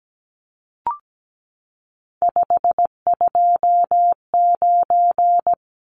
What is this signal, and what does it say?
Morse code, heard as a keyed tone, about 17 wpm, 700 Hz. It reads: 529
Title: Numbers - 3-Digit 17wpm